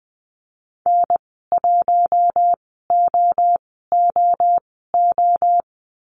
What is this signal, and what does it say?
Morse code, heard as a keyed tone, about 20 wpm, spaced as for 14 wpm, 700 Hz. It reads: N1OOO
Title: 088 Lesson 15g - Number 5 - Call Signs 20wpm